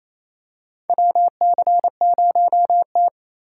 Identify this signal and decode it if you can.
Morse code, heard as a keyed tone, about 28 wpm, 700 Hz. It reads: WC0T
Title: USA Call Signs - 4 Characters - 3x Character Spacing - 28wpm